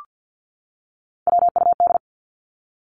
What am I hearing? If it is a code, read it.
Morse code, heard as a keyed tone, about 50 wpm, 700 Hz. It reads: WUD
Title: Abbreviations 50wpm